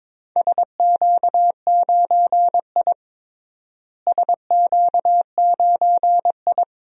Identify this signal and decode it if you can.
Morse code, heard as a keyed tone, about 22 wpm, 700 Hz. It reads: SQ9I SQ9I
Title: DXCC-100 Call Signs - 4 Characters - Repeated 2x - 22wpm